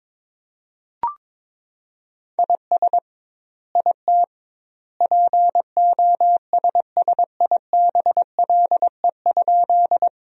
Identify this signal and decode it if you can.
Morse code, heard as a keyed tone, about 22 wpm, 700 Hz. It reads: IS IT POSSIBLE?
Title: 2x Word-Spacing - Sentences from Top 500 Words - 22wpm